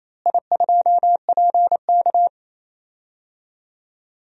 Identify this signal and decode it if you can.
Morse code, heard as a keyed tone, about 28 wpm, 700 Hz. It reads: I2PK
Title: DXCC-100 Call Signs - 4 Characters - Repeated 3x - 28wpm